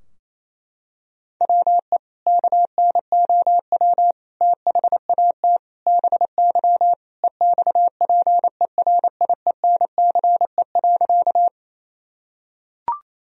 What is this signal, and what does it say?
Morse code, heard as a keyed tone, about 28 wpm, 700 Hz. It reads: WE KNOW THAT BY EXPERIENCE.
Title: Sentences from Top 1000 Words - Review Part 3 - 28wpm